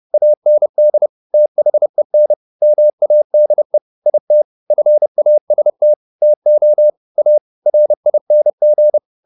Morse code, heard as a keyed tone, about 30 wpm, 600 Hz. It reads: AND THEN MADE IT FAST TO A RING
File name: the old man and the sea / the old man and the sea 1 30wpm